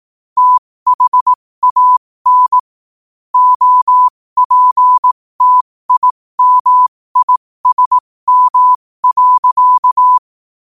Morse code, heard as a keyed tone, about 18 wpm, spaced as for 15 wpm, 1000 Hz. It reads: THAN OPTIMISM.
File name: qotdfiles / 15wpm